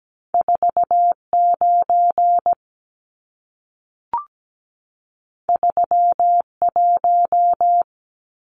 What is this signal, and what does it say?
Morse code, heard as a keyed tone, about 17 wpm, 700 Hz. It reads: 49 31
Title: Numbers - 2-Digit 17wpm